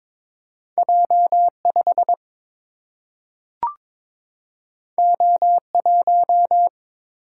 Morse code, heard as a keyed tone, about 21 wpm, 700 Hz. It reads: J5 O1
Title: Single-Letter Single-Number 22wpm